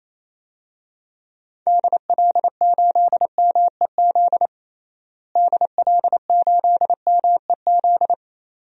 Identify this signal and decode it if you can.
Morse code, heard as a keyed tone, about 28 wpm, 700 Hz. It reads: DL8MEZ DL8MEZ
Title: DXCC-100 Call Signs - 6 Characters - Repeated 2x - 28wpm